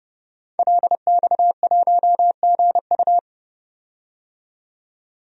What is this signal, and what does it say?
Morse code, heard as a keyed tone, about 32 wpm, 700 Hz. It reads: LX1GU
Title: DXCC-100 Call Signs - 5 Characters - 30wpm